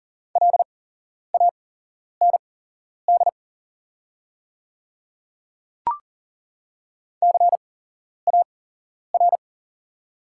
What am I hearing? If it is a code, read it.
Morse code, heard as a keyed tone, about 41 wpm, spaced as for 11 wpm, 690 Hz. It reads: LAND CAR
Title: 8x Character Spacing - Top 300 Words - 40wpm